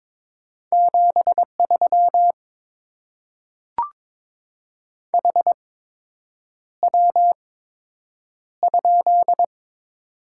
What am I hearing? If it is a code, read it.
Morse code, heard as a keyed tone, about 22 wpm, spaced as for 8 wpm, 700 Hz. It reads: 73 HW?
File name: Top 20 QSO Elements - 8x Character Spacing - 22wpm